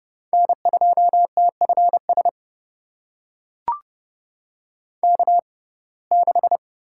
Morse code, heard as a keyed tone, about 29 wpm, spaced as for 22 wpm, 700 Hz. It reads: N2TFS K6
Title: USA Call Signs - 6x Character Spacing - 30wpm